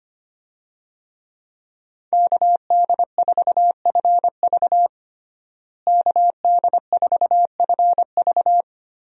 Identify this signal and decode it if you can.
Morse code, heard as a keyed tone, about 25 wpm, 700 Hz. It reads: KD4FV KD4FV
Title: USA Call Signs - 5 Characters - Repeated 2x - 25wpm